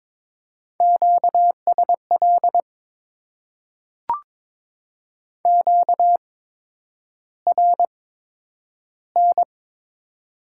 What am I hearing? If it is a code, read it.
Morse code, heard as a keyed tone, about 22 wpm, spaced as for 9 wpm, 700 Hz. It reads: QSL QRN
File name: Top 90 QSO Elements - 8x Character Spacing - 22wpm